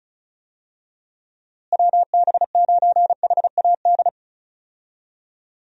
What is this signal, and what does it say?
Morse code, heard as a keyed tone, about 34 wpm, 700 Hz. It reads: WB9HAD
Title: USA Call Signs - 6 Characters - Repeated 3x - 35wpm